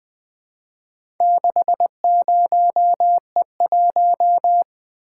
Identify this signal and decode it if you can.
Morse code, heard as a keyed tone, about 20 wpm, 700 Hz. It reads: 60E1
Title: Four Letter-Number Combinations - 20wpm